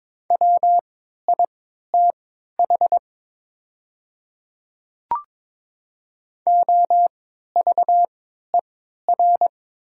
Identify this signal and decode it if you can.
Morse code, heard as a keyed tone, about 22 wpm, spaced as for 13 wpm, 700 Hz. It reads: WITH OVER
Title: 3x Character Spacing - Top 100 Words - 22wpm